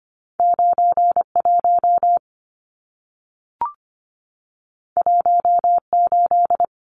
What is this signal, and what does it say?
Morse code, heard as a keyed tone, about 25 wpm, 700 Hz. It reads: 91 18
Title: Numbers - 2-Digit 25wpm